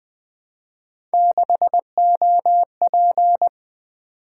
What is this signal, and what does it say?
Morse code, heard as a keyed tone, about 20 wpm, 700 Hz. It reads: 6OP
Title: Three Letter-Number Combinations - 20wpm